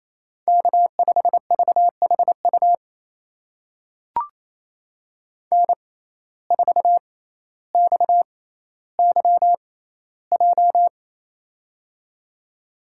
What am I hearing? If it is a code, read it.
Morse code, heard as a keyed tone, about 28 wpm, spaced as for 14 wpm, 700 Hz. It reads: K5VHU N4XYJ
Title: USA Call Signs - 5 Characters - 6x Character Spacing - 28wpm